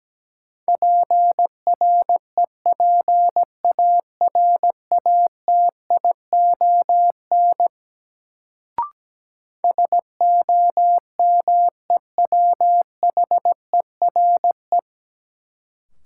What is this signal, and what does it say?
Morse code, heard as a keyed tone, about 17 wpm, 700 Hz. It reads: PREPARATION SOMEWHERE
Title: Sentences from Top 1400 Words - New Word Drill Alt - 17wpm